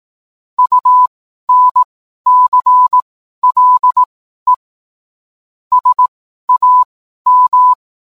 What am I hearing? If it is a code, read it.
Morse code, heard as a keyed tone, about 18 wpm, spaced as for 12 wpm, 1000 Hz. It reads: UNCLE SAM